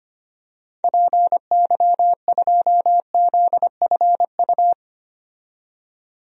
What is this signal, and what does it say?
Morse code, heard as a keyed tone, about 25 wpm, 700 Hz. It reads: PY2ZFU
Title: DXCC-100 Call Signs - 6 Characters - 25wpm